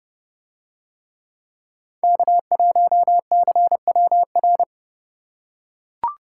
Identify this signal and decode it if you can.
Morse code, heard as a keyed tone, about 28 wpm, 700 Hz. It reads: K1CWR